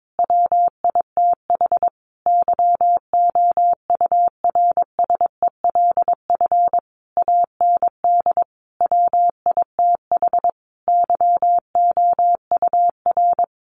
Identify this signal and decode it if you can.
Morse code, heard as a keyed tone, about 22 wpm, 700 Hz. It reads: WITH YOURSELF AND WITH YOUR
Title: Sentences from Top 900 Words 22wpm